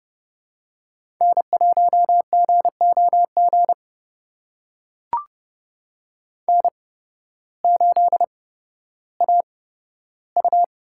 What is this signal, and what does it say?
Morse code, heard as a keyed tone, about 31 wpm, spaced as for 14 wpm, 700 Hz. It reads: N1GOG N8AU
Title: USA Call Signs - 5 Characters - 8x Character Spacing - 30wpm